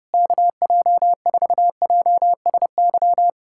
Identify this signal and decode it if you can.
Morse code, heard as a keyed tone, about 30 wpm, 700 Hz. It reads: KJ4JSY
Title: USA Callsign DE Callsign - 6 Characters - 30wpm